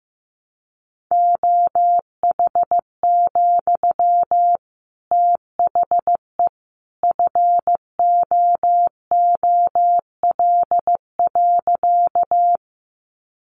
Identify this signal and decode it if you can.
Morse code, heard as a keyed tone, about 15 wpm, 700 Hz. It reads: OH, THE FOOL.
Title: A Sound Of Thunder 15wpm